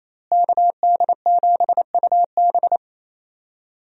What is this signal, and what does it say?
Morse code, heard as a keyed tone, about 28 wpm, 700 Hz. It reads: KD7UB
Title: USA Call Signs - 5 Characters - Repeated 2x - 28wpm